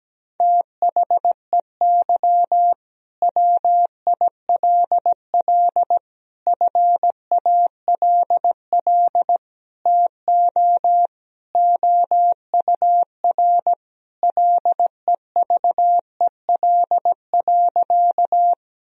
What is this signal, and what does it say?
Morse code, heard as a keyed tone, about 17 wpm, 700 Hz. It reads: THEY WILL FALL TO OUR LEVEL.
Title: Sentences from Top 1000 Words - Review Part 4 - 17wpm